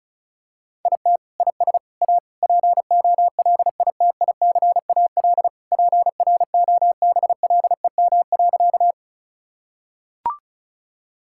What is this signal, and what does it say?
Morse code, heard as a keyed tone, about 36 wpm, 700 Hz. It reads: IT IS A POLITICAL PROBLEM.
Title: Sentences from Top 1000 Words - Review Part 5 - 35wpm